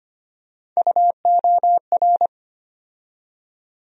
Morse code, heard as a keyed tone, about 25 wpm, 700 Hz. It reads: UOR
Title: USA Call Sign Postfixes - 3 Letters 25wpm